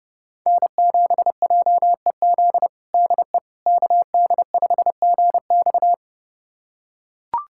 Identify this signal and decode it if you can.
Morse code, heard as a keyed tone, about 30 wpm, 700 Hz. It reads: N7JEZ DE KD5GX
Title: USA Callsign DE Callsign - 5 Characters - 30wpm